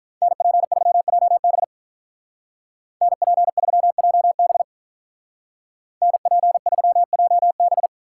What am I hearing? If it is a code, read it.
Morse code, heard as a keyed tone, about 43 wpm, 700 Hz. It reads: NP3JB NP3JB NP3JB
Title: Speed Racing - USA Call Signs 35wpm